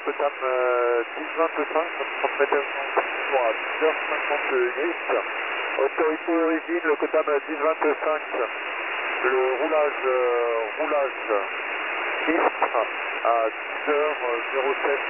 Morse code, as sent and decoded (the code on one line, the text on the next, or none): none